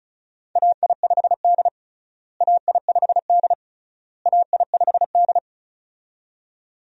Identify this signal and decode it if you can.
Morse code, heard as a keyed tone, about 36 wpm, 700 Hz. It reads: AI5D AI5D AI5D
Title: USA Call Signs - 4 Characters - Repeated 3x - 35wpm